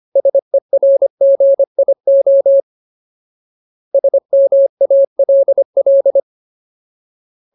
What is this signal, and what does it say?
Morse code, heard as a keyed tone, about 25 wpm, 550 Hz. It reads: SERGIO SMALL
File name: Ultimate Word List / 25WPM Ultimate